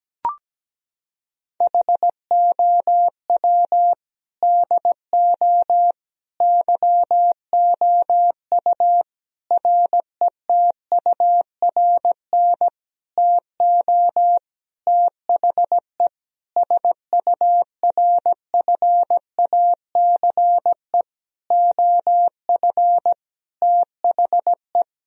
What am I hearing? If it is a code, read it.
Morse code, heard as a keyed tone, about 17 wpm, 700 Hz. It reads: HOW DO YOU RETURN TO THE SURFACE OF THE
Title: Sentences from Top 1000 Words - Review Part 3 - 17wpm